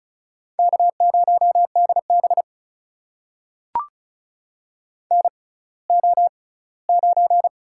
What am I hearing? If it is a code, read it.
Morse code, heard as a keyed tone, about 36 wpm, spaced as for 20 wpm, 700 Hz. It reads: K0DB NO9
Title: USA Call Signs - 4 Characters - 6x Character Spacing - 35wpm